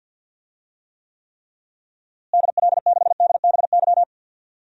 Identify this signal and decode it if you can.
Morse code, heard as a keyed tone, about 52 wpm, 700 Hz. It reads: DL6DBY
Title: DXCC-100 Call Signs - 6 Characters - Repeated 3x - 50wpm